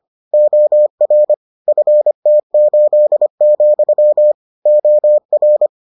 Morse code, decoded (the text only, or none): OR FT8, OR